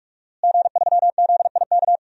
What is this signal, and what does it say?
Morse code, heard as a keyed tone, about 46 wpm, 700 Hz. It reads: G3ZIK
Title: DXCC-100 Call Signs - 5 Characters - Repeated 2x - 45wpm